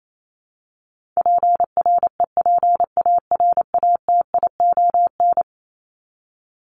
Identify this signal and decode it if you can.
Morse code, heard as a keyed tone, about 28 wpm, 700 Hz. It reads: PREPARATION
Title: Top 1251-1500 Words 28wpm